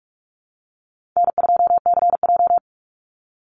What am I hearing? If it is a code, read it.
Morse code, heard as a keyed tone, about 44 wpm, 700 Hz. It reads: N2CJ
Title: Call Signs - 45wpm